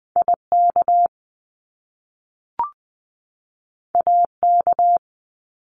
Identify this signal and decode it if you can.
Morse code, heard as a keyed tone, about 20 wpm, 700 Hz. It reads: IK AK
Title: Top Two Letter Combinations in Words 20wpm